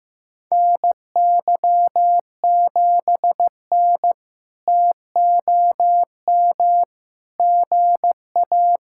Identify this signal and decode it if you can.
Morse code, heard as a keyed tone, about 15 wpm, 700 Hz. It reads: NY7N TOM GA
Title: K1USN SST QSO Practice - 15wpm